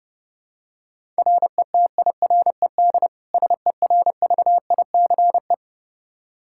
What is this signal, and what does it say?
Morse code, heard as a keyed tone, about 29 wpm, 700 Hz. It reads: RETIRED SERVICE
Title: Practice QSO - Exchange 3 - More Info - 30wpm